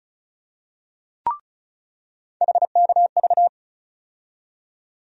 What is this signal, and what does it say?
Morse code, heard as a keyed tone, about 34 wpm, 700 Hz. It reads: HKV
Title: USA Call Sign Postfixes - 3 Letters 35wpm